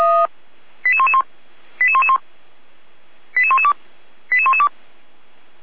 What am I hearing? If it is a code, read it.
Five-tone selective call (ZVEI), heard as quick runs of five tones, tones of about 70 ms. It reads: 88111 88111 88112 88112